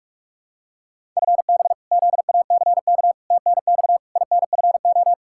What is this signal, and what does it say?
Morse code, heard as a keyed tone, about 47 wpm, 700 Hz. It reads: FB ZACK TNX INFO
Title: Practice QSO - Exchange 3 - More Info - 45wpm